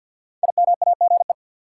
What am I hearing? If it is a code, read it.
Morse code, heard as a keyed tone, about 47 wpm, 700 Hz. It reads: IMAGE
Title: Five Letter Words - 50wpm